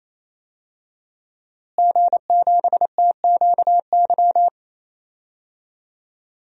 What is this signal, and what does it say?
Morse code, heard as a keyed tone, about 28 wpm, 700 Hz. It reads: G7TQY
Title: DXCC-100 Call Signs - 5 Characters - Repeated 3x - 28wpm